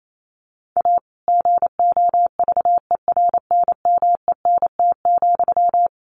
Morse code, heard as a keyed tone, about 28 wpm, 700 Hz. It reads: A GOVERNMENT,
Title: Sentences from Top 1400 Words 28wpm